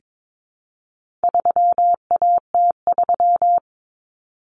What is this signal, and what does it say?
Morse code, heard as a keyed tone, about 22 wpm, 700 Hz. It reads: 3AT3
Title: Common Cut Numbers - 4-Digit 22wpm